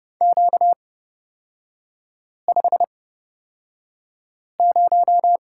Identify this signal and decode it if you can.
Morse code, heard as a keyed tone, about 30 wpm, spaced as for 5 wpm, 700 Hz. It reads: Q50
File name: Lesson 38e - Letter q - Three Characters 30wpm